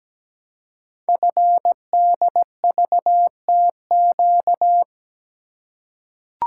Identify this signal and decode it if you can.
Morse code, heard as a keyed tone, about 17 wpm, 700 Hz. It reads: FDVTQ
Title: Code Groups - 5 Letters - 17wpm